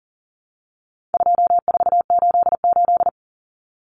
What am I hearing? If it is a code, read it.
Morse code, heard as a keyed tone, about 40 wpm, 700 Hz. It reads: 2488
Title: Numbers - 4-Digit 40wpm